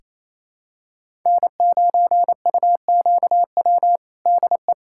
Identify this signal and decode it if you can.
Morse code, heard as a keyed tone, about 28 wpm, 700 Hz. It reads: N9UQW DE
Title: USA Callsign DE Callsign - 5 Characters - 28wpm